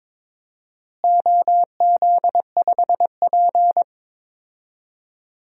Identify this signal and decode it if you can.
Morse code, heard as a keyed tone, about 22 wpm, 700 Hz. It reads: OZ5P
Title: DXCC-100 Call Signs - 4 Characters - Repeated 3x - 22wpm